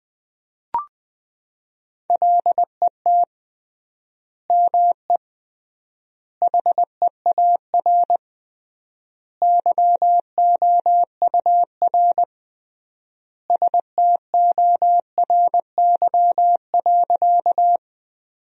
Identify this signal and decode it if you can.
Morse code, heard as a keyed tone, about 20 wpm, 700 Hz. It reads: LET ME HEAR YOUR STORY.
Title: 3x Word-Spacing - Sentences from Top 300 Words - 20wpm